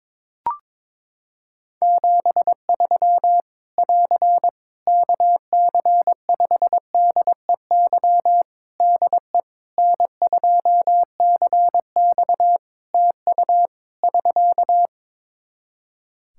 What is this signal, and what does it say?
Morse code, heard as a keyed tone, about 22 wpm, 700 Hz. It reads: 73 <AR> KC5DEY DE N2CX TU <SK>
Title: Practice QSO - Exchange 4 - Ending - 22wpm